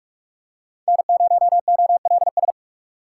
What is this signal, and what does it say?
Morse code, heard as a keyed tone, about 46 wpm, 700 Hz. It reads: N0OLS
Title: USA Call Signs - 5 Characters - 2x Character Spacing - 45wpm